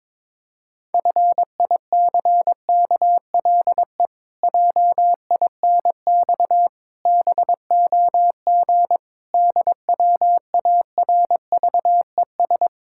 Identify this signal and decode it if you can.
Morse code, heard as a keyed tone, about 22 wpm, 700 Hz. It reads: FICKLE JINX BOG DWARVES
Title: Pangram Challenge v2 - 22wpm